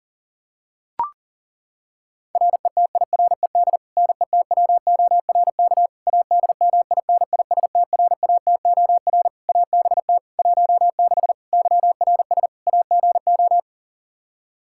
Morse code, heard as a keyed tone, about 40 wpm, 700 Hz. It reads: RETIRED NETWORK ADMINISTRATOR ABT 16 YRS AGO
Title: Practice QSO - Exchange 3 - More Info - 40wpm